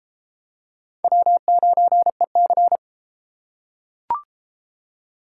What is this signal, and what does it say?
Morse code, heard as a keyed tone, about 31 wpm, 700 Hz. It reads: W9EC